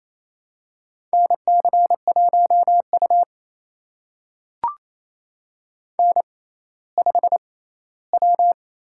Morse code, heard as a keyed tone, about 28 wpm, spaced as for 16 wpm, 700 Hz. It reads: NC1U N5W
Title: USA Call Signs - 6x Character Spacing - 28wpm